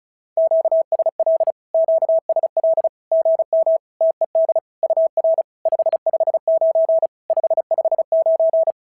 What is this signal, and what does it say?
Morse code, heard as a keyed tone, about 34 wpm, 650 Hz. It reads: QSL QSL GM TED UR 559 559
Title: SOTA QSO Practice 35wpm